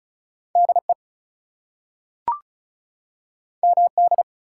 Morse code, heard as a keyed tone, about 36 wpm, 700 Hz. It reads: DE MD
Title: US States and Canadian Provinces 35wpm